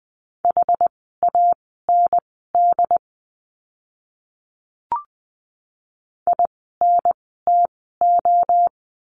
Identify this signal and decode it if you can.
Morse code, heard as a keyed tone, about 20 wpm, spaced as for 14 wpm, 700 Hz. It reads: HAND INTO
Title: 237 Lesson 38f - Letter q - Words 20wpm